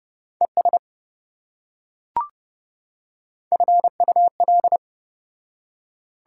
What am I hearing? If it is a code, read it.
Morse code, heard as a keyed tone, about 27 wpm, 700 Hz. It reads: ES FUL